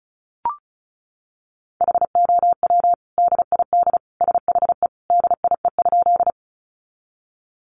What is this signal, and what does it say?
Morse code, heard as a keyed tone, about 34 wpm, 700 Hz. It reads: HOW DID SHE DIE?
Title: Sentences from Top 1300 Words 35wpm